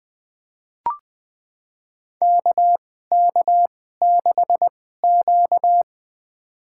Morse code, heard as a keyed tone, about 20 wpm, spaced as for 15 wpm, 700 Hz. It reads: KK6Q
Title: USA Call Signs - 4 Characters - 2x Character Spacing - 20wpm